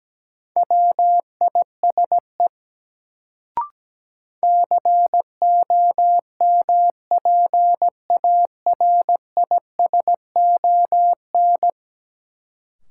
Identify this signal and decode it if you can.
Morse code, heard as a keyed tone, about 17 wpm, 700 Hz. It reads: WISE COMPARISON
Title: Sentences from Top 1600 Words - New Word Drill - 17wpm